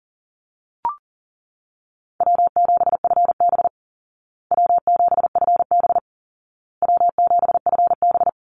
Morse code, heard as a keyed tone, about 40 wpm, 700 Hz. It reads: W7FB W7FB W7FB
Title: Easy Call Signs Repeated 3x - 40wpm